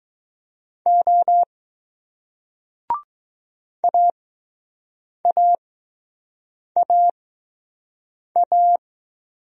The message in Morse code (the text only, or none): O AAAA